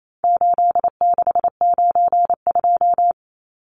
Morse code, heard as a keyed tone, about 28 wpm, 700 Hz. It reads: 8692
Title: Numbers - 4-Digit 28wpm